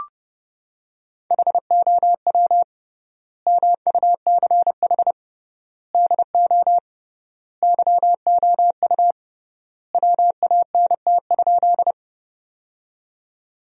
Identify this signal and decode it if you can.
Morse code, heard as a keyed tone, about 30 wpm, 700 Hz. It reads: HOW MUCH DO YOU WANT?
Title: 3x Word-Spacing - Sentences from Top 200 Words - 30wpm